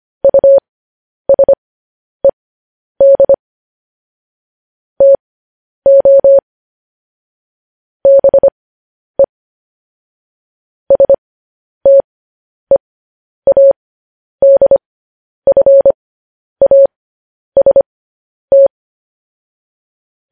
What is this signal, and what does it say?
Morse code, heard as a keyed tone, about 25 wpm, spaced as for 10 wpm, 550 Hz. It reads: USED TO BE STEADFAST
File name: podcast / News headlines-1x-25x10